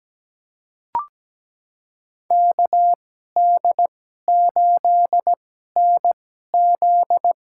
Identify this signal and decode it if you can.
Morse code, heard as a keyed tone, about 17 wpm, spaced as for 12 wpm, 700 Hz. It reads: KD8NZ
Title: USA Call Signs - 6 Characters - 2x Character Spacing - 17wpm